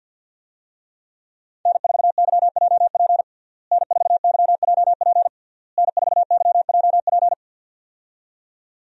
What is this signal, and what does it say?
Morse code, heard as a keyed tone, about 52 wpm, 700 Hz. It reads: N4YJP N4YJP N4YJP
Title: USA Call Signs - 5 Characters - Repeated 3x - 50wpm